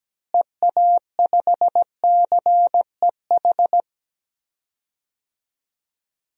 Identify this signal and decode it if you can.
Morse code, heard as a keyed tone, about 17 wpm, 700 Hz. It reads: EA5CEH
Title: DXCC-100 Call Signs - 6 Characters - Repeated 2x - 17wpm